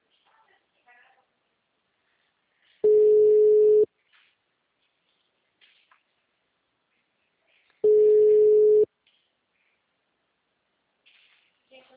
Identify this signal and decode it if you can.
Ringback tone (Europe, 425 Hz), with speech.